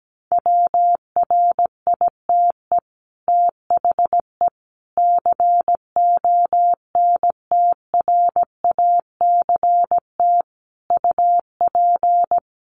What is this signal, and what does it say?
Morse code, heard as a keyed tone, about 17 wpm, 700 Hz. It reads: WRITE THE CONTRACT UP
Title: Sentences from Top 1400 Words 17wpm